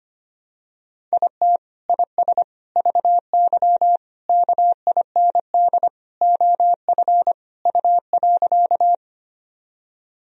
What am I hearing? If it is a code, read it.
Morse code, heard as a keyed tone, about 25 wpm, 700 Hz. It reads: IT IS VY KIND OF U.
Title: Sentences from Top 200 Words and Abbreviations 25wpm